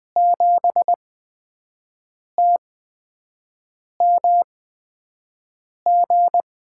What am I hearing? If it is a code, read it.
Morse code, heard as a keyed tone, about 21 wpm, spaced as for 5 wpm, 700 Hz. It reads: 7TMG
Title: USA Call Signs - 6 Characters - 8x Character Spacing - 20wpm